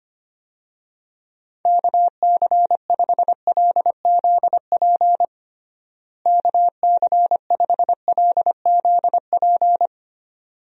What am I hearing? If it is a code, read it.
Morse code, heard as a keyed tone, about 25 wpm, 700 Hz. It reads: KC5LZP KC5LZP
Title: USA Call Signs - 6 Characters - Repeated 2x - 25wpm